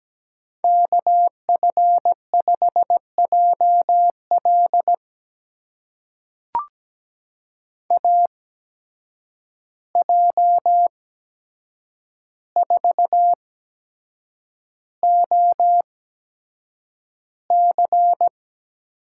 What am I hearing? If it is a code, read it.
Morse code, heard as a keyed tone, about 17 wpm, spaced as for 7 wpm, 700 Hz. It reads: KF5JL AJ4OC
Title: USA Call Signs - 5 Characters - 8x Character Spacing - 17wpm